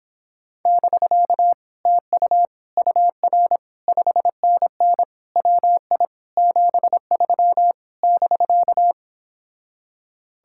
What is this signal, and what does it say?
Morse code, heard as a keyed tone, about 31 wpm, 700 Hz. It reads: <BK> TU UR 5NN WI 73 <BK>